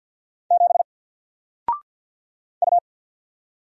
Morse code, heard as a keyed tone, about 48 wpm, 700 Hz. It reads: 7 U
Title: Single Character (Letter:Number:Punctuation:Prosign) Alt 50wpm